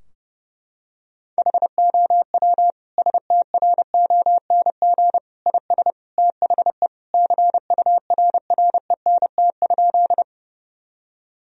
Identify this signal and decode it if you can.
Morse code, heard as a keyed tone, about 30 wpm, 700 Hz. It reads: HOW STRONG IS THE CURRENT?
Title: Sentences from Top 1000 Words - Review Part 4 - 30wpm